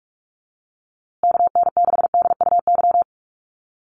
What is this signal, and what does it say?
Morse code, heard as a keyed tone, about 44 wpm, 700 Hz. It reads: KN6DUY
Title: USA Call Signs Repeated 3x 45wpm